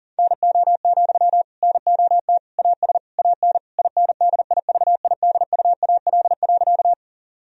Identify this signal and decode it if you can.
Morse code, heard as a keyed tone, about 40 wpm, 700 Hz. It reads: NO, NOT AS AN INDIVIDUAL.
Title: Sentences from Top 1000 Words - Review Part 4 - 40wpm